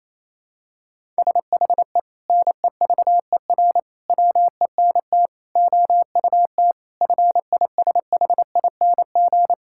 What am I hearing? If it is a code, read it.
Morse code, heard as a keyed tone, about 28 wpm, 700 Hz. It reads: SHE NEVER WENT OUT FISHING
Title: Sentences from Top 1000 Words - Review Part 5 - 28wpm